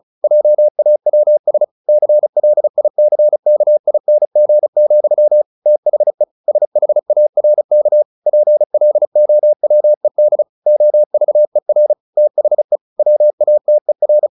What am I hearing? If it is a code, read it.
Morse code, heard as a keyed tone, about 34 wpm, 600 Hz. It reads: JAWS CLICKING, THE SHARK PLOWED OVER THE WATER
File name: the old man and the sea 2 35wpm